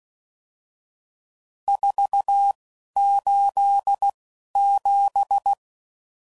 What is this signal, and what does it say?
Morse code, heard as a keyed tone, about 16 wpm, spaced as for 11 wpm, 800 Hz. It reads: 487
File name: CW Code Groups @10WPM/RSTs/Various RST No's